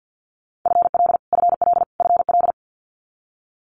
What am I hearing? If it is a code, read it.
Morse code, heard as a keyed tone, about 50 wpm, 700 Hz. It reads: FL FL FL
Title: Multi-letter Phonemes Repeated 3x 50wpm